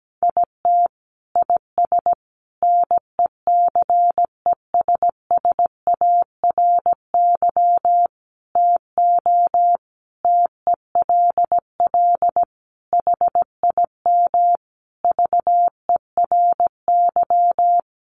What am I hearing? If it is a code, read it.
Morse code, heard as a keyed tone, about 17 wpm, 700 Hz. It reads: IT IS NECESSARY TO TELL HIM VERY
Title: Sentences from Top 800 Words 17wpm